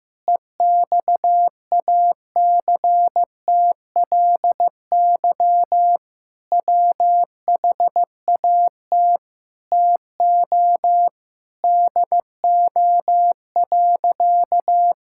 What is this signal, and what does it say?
Morse code, heard as a keyed tone, about 15 wpm, 700 Hz. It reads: EXACTLY WHAT TO DO.
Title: Sentences from Top 1000 Words - Review Part 6 - 15wpm